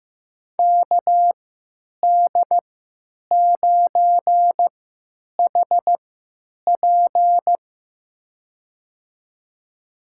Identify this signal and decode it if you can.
Morse code, heard as a keyed tone, about 16 wpm, spaced as for 9 wpm, 700 Hz. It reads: KD9HP
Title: USA Call Signs - 5 Characters - 3x Character Spacing - 15wpm